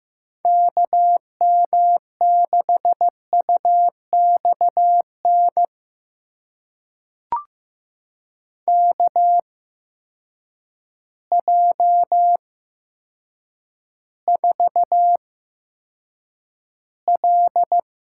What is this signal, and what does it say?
Morse code, heard as a keyed tone, about 15 wpm, spaced as for 8 wpm, 700 Hz. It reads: KM6UXN KJ4L
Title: USA Call Signs - 6 Characters - 8x Character Spacing - 15wpm